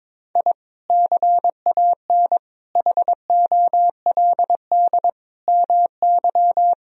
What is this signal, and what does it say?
Morse code, heard as a keyed tone, about 22 wpm, 700 Hz. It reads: I CAN HOLD MY
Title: Sentences from Top 1000 Words - Review Part 2 - 22wpm